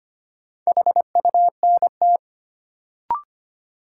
Morse code, heard as a keyed tone, about 25 wpm, 700 Hz. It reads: HUNT
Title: Sentences from Top 1600 Words - New Word Drill - 25wpm